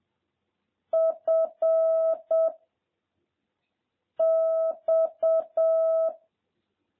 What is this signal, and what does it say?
Morse code, heard as a keyed tone, about 7 wpm, 650 Hz. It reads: F X